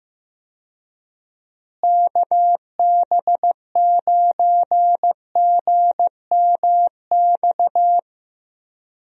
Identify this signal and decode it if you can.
Morse code, heard as a keyed tone, about 15 wpm, 700 Hz. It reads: KB9GMX
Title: USA Call Signs - 6 Characters - Repeated 3x - 15wpm